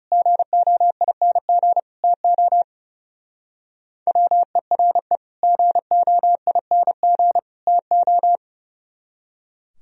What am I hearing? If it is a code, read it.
Morse code, heard as a keyed tone, about 35 wpm, 690 Hz. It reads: GOING TO WERE GOING TO
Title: Speed-Racing - Sets of 3 Words 30wpm